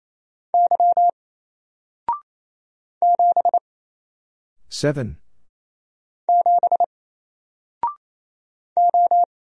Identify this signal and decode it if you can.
Morse code, heard as a keyed tone, about 27 wpm, 700 Hz. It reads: Y 7 7 O
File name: Single-Letter-Number-28wpm